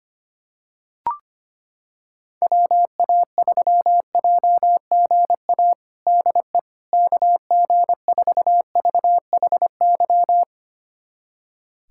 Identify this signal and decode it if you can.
Morse code, heard as a keyed tone, about 25 wpm, 700 Hz. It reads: WA3JGA DE KG4VHY
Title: USA Callsign DE Callsign 25wpm